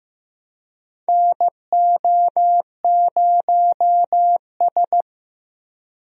Morse code, heard as a keyed tone, about 15 wpm, 700 Hz. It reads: NO0S
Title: USA Call Signs - 4 Characters - Repeated 3x - 15wpm